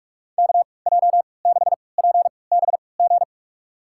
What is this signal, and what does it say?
Morse code, heard as a keyed tone, about 42 wpm, spaced as for 25 wpm, 700 Hz. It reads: KJ6PBG
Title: USA Call Signs - 6 Characters - 3x Character Spacing - 45wpm